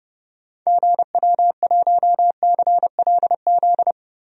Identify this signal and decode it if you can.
Morse code, heard as a keyed tone, about 30 wpm, 700 Hz. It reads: GW1CLZ
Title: DXCC-100 Call Signs - 6 Characters - Repeated 3x - 30wpm